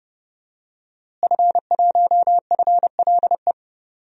Morse code, heard as a keyed tone, about 30 wpm, 700 Hz. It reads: F1FLE
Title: DXCC-100 Call Signs - 5 Characters - Repeated 2x - 30wpm